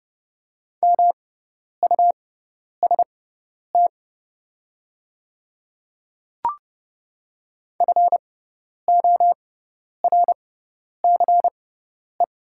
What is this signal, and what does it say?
Morse code, heard as a keyed tone, about 30 wpm, spaced as for 10 wpm, 700 Hz. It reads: MUST FORCE
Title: 6x Character Spacing - Top 500 Words - 30wpm